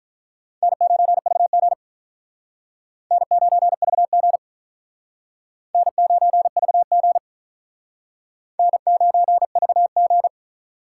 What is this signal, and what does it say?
Morse code, heard as a keyed tone, about 46 wpm, 700 Hz. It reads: N9VG N9VG N9VG N9VG